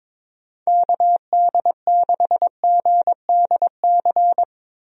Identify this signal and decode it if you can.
Morse code, heard as a keyed tone, about 22 wpm, 700 Hz. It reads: KD6GDC
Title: USA Call Signs - 6 Characters - Repeated 2x - 22wpm